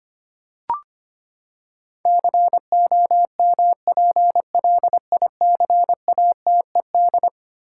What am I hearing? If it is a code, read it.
Morse code, heard as a keyed tone, about 25 wpm, 700 Hz. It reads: COMPLICATED
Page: morse-code-ninja-practice-25wpm